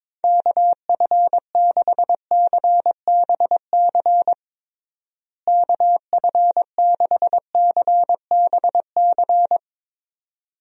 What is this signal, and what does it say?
Morse code, heard as a keyed tone, about 22 wpm, 700 Hz. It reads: KF6CBC KF6CBC
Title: USA Call Signs - 6 Characters - Repeated 2x - 22wpm